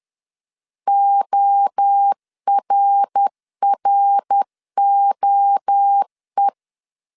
Morse code, heard as a keyed tone, about 10 wpm, 800 Hz. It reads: ORROE
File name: CW / DET / 1 3 1500C S B 10wpm 3335